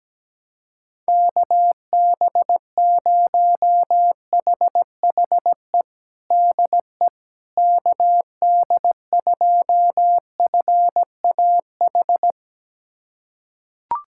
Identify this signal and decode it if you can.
Morse code, heard as a keyed tone, about 17 wpm, 700 Hz. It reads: KB0HHE DE KD2FAH
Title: USA Callsign DE Callsign 17wpm